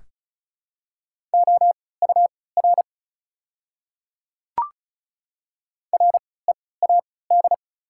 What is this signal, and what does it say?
Morse code, heard as a keyed tone, about 36 wpm, spaced as for 20 wpm, 700 Hz. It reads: OUR READ
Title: 3x Character Spacing - Top 200 Words - 35wpm